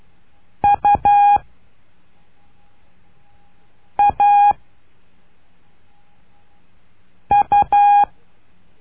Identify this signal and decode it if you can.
Morse code, heard as a keyed tone, about 12 wpm, 820 Hz. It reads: U A U
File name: cursos / CURSO DE TELEGRAFIA / Telegrafia - Trilha 01